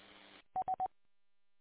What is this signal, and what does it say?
There is no code.